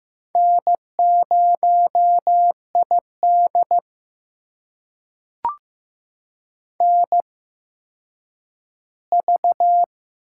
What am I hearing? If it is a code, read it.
Morse code, heard as a keyed tone, about 15 wpm, spaced as for 9 wpm, 700 Hz. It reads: N0ID NV
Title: USA Call Signs - 4 Characters - 8x Character Spacing - 15wpm